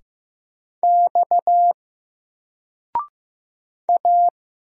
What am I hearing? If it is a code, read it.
Morse code, heard as a keyed tone, about 15 wpm, 700 Hz. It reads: X A
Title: Single Letter-Number 15wpm